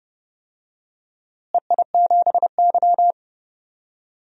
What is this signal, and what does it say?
Morse code, heard as a keyed tone, about 30 wpm, 700 Hz. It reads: EI7Y
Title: DXCC-100 Call Signs - 4 Characters - Repeated 2x - 30wpm